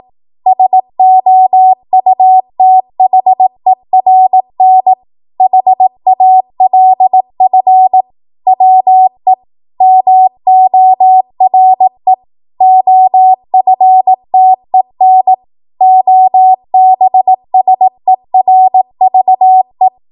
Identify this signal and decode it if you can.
Morse code, heard as a keyed tone, about 18 wpm, 750 Hz. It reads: SOUTHERN HALF WE MORE OFTEN OBSERVE
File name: Morse / 18WPM